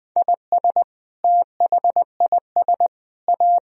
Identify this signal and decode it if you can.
Morse code, heard as a keyed tone, about 21 wpm, 700 Hz. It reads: IS THIS A
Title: Sentences from Top 1000 Words - Review Part 4 - 20wpm